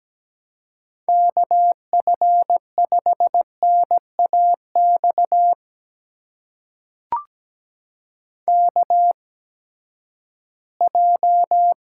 Morse code, heard as a keyed tone, about 17 wpm, spaced as for 12 wpm, 700 Hz. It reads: KF5NAX KJ
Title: USA Call Signs - 6 Characters - 8x Character Spacing - 17wpm